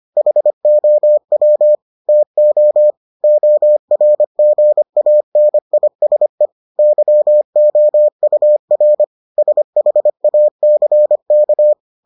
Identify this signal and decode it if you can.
Morse code, heard as a keyed tone, about 25 wpm, 600 Hz. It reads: HOW TO ORGANISE YOUR SHACK